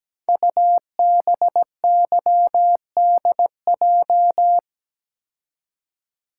Morse code, heard as a keyed tone, about 17 wpm, 700 Hz. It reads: UBYDJ
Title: Code Groups - 5 Letters - 17wpm